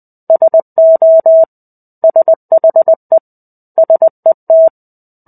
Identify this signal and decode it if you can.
Morse code, heard as a keyed tone, about 20 wpm, 650 Hz. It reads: SO SHE SET
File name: 05-Chapter-05 20 WPM